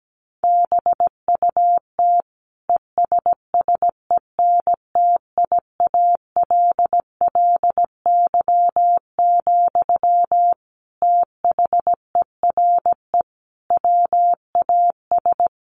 Morse code, heard as a keyed tone, about 17 wpm, 700 Hz. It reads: BUT ESSENTIALLY, THERE WAS
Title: Sentences from Top 1300 Words 17wpm